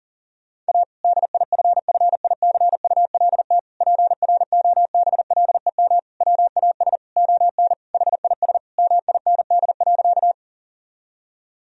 Morse code, heard as a keyed tone, about 40 wpm, 700 Hz. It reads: A DIFFICULT PROBLEM WAS ON HIS MIND.
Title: Sentences from Top 1000 Words - Review Part 2 - 40wpm